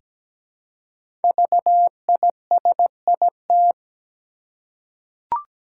Morse code, heard as a keyed tone, about 17 wpm, 700 Hz. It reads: VISIT
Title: Five Letter Words - 17wpm